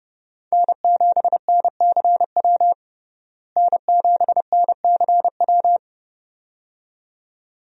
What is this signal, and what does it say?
Morse code, heard as a keyed tone, about 32 wpm, 700 Hz. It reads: N7NCW N7NCW
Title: USA Call Signs - 5 Characters - Repeated 2x - 30wpm